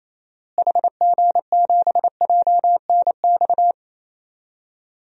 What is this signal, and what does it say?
Morse code, heard as a keyed tone, about 28 wpm, 700 Hz. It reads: HG7JNX
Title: DXCC-100 Call Signs - 6 Characters - Repeated 3x - 28wpm